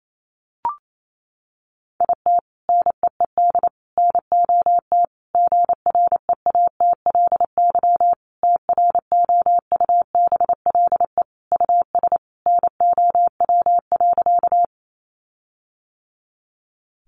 Morse code, heard as a keyed tone, about 28 wpm, 700 Hz. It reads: IT NEED NOT GREATLY TROUBLE US NOW.
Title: Sentences from Top 1400 Words 28wpm